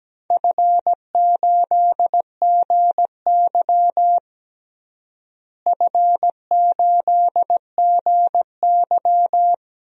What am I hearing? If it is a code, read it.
Morse code, heard as a keyed tone, about 17 wpm, 700 Hz. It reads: F8GY F8GY
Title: DXCC-100 Call Signs - 4 Characters - Repeated 2x - 17wpm